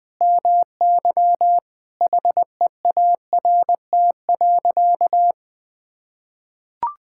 Morse code, heard as a keyed tone, about 21 wpm, 700 Hz. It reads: MY HEART.
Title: Sentences with 6-Word Phrases - 20wpm